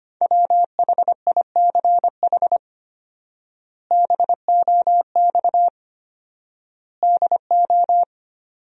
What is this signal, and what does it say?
Morse code, heard as a keyed tone, about 25 wpm, 700 Hz. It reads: WHICH BOX DO
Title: 4x Word-Spacing - Sentences from Top 500 Words - 25wpm